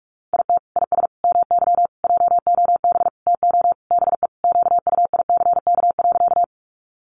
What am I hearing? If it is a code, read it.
Morse code, heard as a keyed tone, about 44 wpm, 700 Hz. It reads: IT IS MY JOB TO BE QUICK.
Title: Sentences from Top 600 Words 45wpm